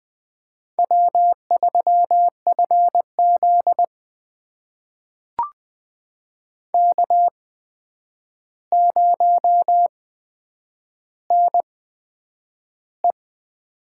Morse code, heard as a keyed tone, about 20 wpm, spaced as for 9 wpm, 700 Hz. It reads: W3FZ K0NE
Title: USA Call Signs - 4 Characters - 8x Character Spacing - 20wpm